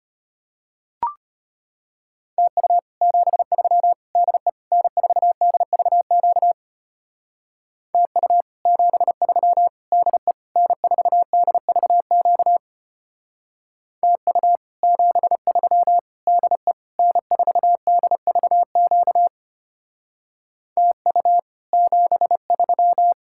Morse code, heard as a keyed tone, about 30 wpm, 700 Hz. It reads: TU 73 DE N4DVQ TU 73 DE N4DVQ TU 73 DE N4DVQ TU 73
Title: Speed-Racing Common QSO Phrases - 25wpm